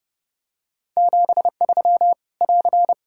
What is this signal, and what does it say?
Morse code, heard as a keyed tone, about 28 wpm, 700 Hz. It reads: 73 <AR>
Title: Practice QSO - Exchange 4 - Ending - 30wpm